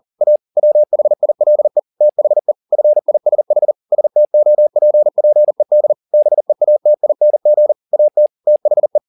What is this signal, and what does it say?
Morse code, heard as a keyed tone, about 40 wpm, 600 Hz. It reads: A WHILE THE FISH STOPPED BEATING AT THE
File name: the old man and the sea 2 40wpm